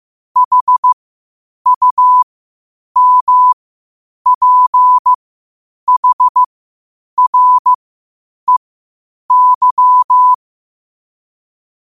Morse code, heard as a keyed tone, about 15 wpm, spaced as for 8 wpm, 1000 Hz. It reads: HUMPHREY
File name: qotdfiles / 8wpm